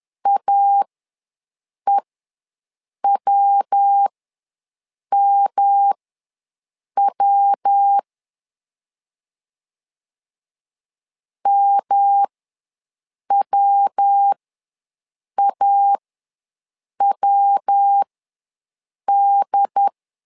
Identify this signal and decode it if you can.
Morse code, heard as a keyed tone, about 10 wpm, spaced as for 6 wpm, 800 Hz. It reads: AEWMW MWAWD
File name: LEZIONI / 3 MWRO / 3 MW ws 10wpm 2213